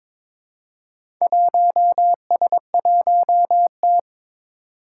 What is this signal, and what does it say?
Morse code, heard as a keyed tone, about 22 wpm, 700 Hz. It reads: 1S1T
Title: Four Challenging Letter-Number Combinations 22wpm